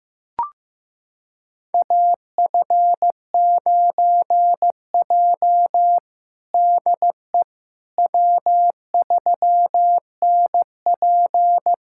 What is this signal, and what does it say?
Morse code, heard as a keyed tone, about 16 wpm, 700 Hz. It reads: AF9J DE W3NP
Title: USA Callsign DE Callsign - 4 Characters - 15wpm